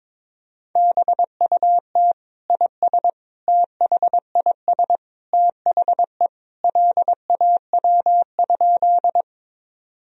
Morse code, heard as a keyed tone, about 22 wpm, 700 Hz. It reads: BUT IS THIS THE LAW?
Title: Sentences from Top 1000 Words - Review Part 3 - 22wpm